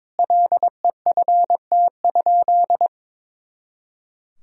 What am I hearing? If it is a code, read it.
Morse code, heard as a keyed tone, about 22 wpm, 700 Hz. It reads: LEFT?
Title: Sentences from Top 300 Words and Abbreviations 22wpm